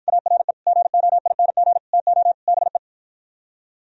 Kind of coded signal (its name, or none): Morse code